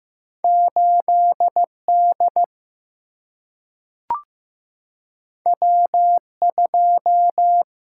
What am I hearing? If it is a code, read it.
Morse code, heard as a keyed tone, about 15 wpm, 700 Hz. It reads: 8D W2
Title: Single-Letter Single-Number 15wpm